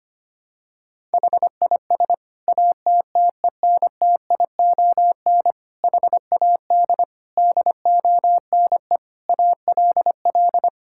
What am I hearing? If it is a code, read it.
Morse code, heard as a keyed tone, about 25 wpm, 700 Hz. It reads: HIS ATTENTION HAD DONE ALL
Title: Sentences from Top 1000 Words - Review Part 6 - 25wpm